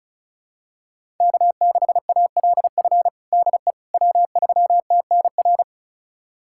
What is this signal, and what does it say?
Morse code, heard as a keyed tone, about 34 wpm, 700 Hz. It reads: K6ALF DE W3TNR
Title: USA Callsign DE Callsign - 5 Characters - 35wpm